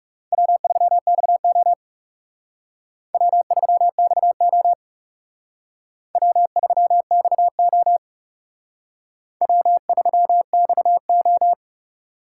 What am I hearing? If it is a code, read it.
Morse code, heard as a keyed tone, about 37 wpm, 700 Hz. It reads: W3XO W3XO W3XO W3XO
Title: Speed Racing - USA 1x2 Call Signs 30wpm